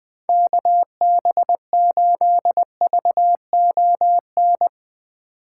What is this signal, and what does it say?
Morse code, heard as a keyed tone, about 20 wpm, 700 Hz. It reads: KB8VON
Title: 2023 POTA Hunter Call Signs - 20wpm